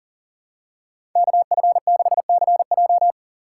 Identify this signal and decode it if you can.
Morse code, heard as a keyed tone, about 40 wpm, 700 Hz. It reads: KF6CJ
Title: USA Call Signs - 5 Characters - 3x Character Spacing - 40wpm